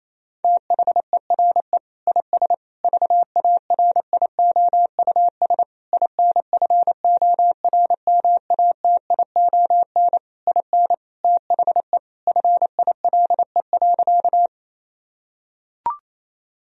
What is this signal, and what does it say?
Morse code, heard as a keyed tone, about 27 wpm, 700 Hz. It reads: THERE IS VARIOUS INFORMATION IN THE FILE.
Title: Sentences from Top 1000 Words - Review Part 4 - 28wpm